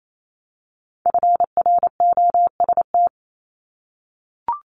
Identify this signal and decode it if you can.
Morse code, heard as a keyed tone, about 28 wpm, 700 Hz. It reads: FROST
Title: Multi-letter Phonemes Repeated 3x 28wpm